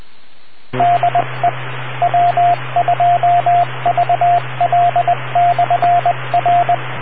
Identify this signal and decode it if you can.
Morse code, heard as a keyed tone, about 20 wpm, 700 Hz, with radio noise. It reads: DE W2VL/R